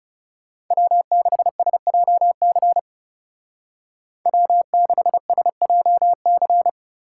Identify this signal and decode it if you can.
Morse code, heard as a keyed tone, about 40 wpm, 700 Hz. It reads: W6SJC W6SJC